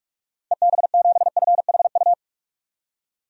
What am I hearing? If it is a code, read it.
Morse code, heard as a keyed tone, about 46 wpm, 700 Hz. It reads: EB7FHU
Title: DXCC-100 Call Signs - 6 Characters - 45wpm